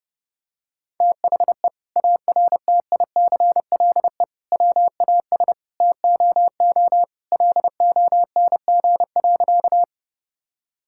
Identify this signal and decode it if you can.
Morse code, heard as a keyed tone, about 32 wpm, 700 Hz. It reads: THE ARTICLE WAS TOO LONG.
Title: Sentences from Top 1000 Words - Review Part 5 - 30wpm